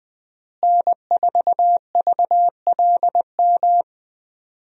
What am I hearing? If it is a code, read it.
Morse code, heard as a keyed tone, about 21 wpm, 700 Hz. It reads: N4VLM
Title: USA Call Signs - 5 Characters - Repeated 2x - 20wpm